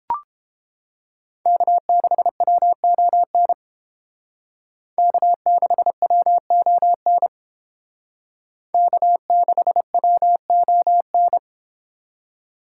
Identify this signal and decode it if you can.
Morse code, heard as a keyed tone, about 30 wpm, 700 Hz. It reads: K6WON K6WON K6WON